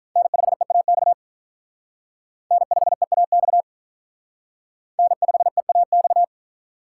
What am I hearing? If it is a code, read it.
Morse code, heard as a keyed tone, about 43 wpm, 700 Hz. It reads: N5EAX N5EAX N5EAX
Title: Speed Racing - USA 1x3 Call Signs 35wpm